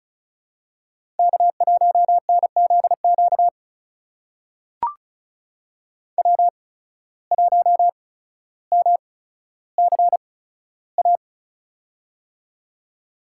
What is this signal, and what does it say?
Morse code, heard as a keyed tone, about 36 wpm, spaced as for 15 wpm, 700 Hz. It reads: K1NZQ W1MCA
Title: USA Call Signs - 5 Characters - 8x Character Spacing - 35wpm